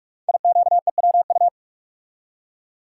Morse code, heard as a keyed tone, about 44 wpm, 700 Hz. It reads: IQEWU
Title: Code Groups - 5 Letters - 45wpm